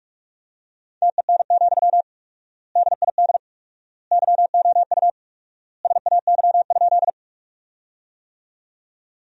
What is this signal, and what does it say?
Morse code, heard as a keyed tone, about 46 wpm, 700 Hz. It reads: TEN, DID YOU SAY?
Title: 4x Word-Spacing - Sentences from Top 400 Words - 45wpm